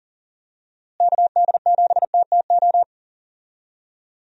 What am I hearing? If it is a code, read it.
Morse code, heard as a keyed tone, about 37 wpm, 700 Hz. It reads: KD7TTO
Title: USA Call Signs - 6 Characters - Repeated 3x - 40wpm